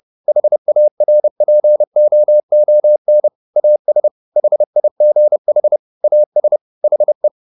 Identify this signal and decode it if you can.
Morse code, heard as a keyed tone, about 30 wpm, 600 Hz. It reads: HARPOON AS HIGH AS HE